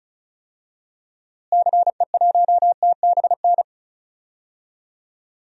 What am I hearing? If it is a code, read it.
Morse code, heard as a keyed tone, about 35 wpm, 700 Hz. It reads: CE1TBN
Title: DXCC-100 Call Signs - 6 Characters - Repeated 3x - 35wpm